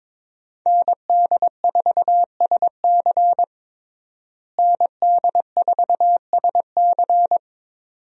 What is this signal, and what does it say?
Morse code, heard as a keyed tone, about 22 wpm, 700 Hz. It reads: ND4SC ND4SC
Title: USA Call Signs - 5 Characters - Repeated 2x - 22wpm